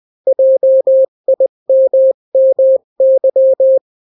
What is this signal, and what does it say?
Morse code, heard as a keyed tone, about 20 wpm, 530 Hz. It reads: JIMMY
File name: daily / 18wpm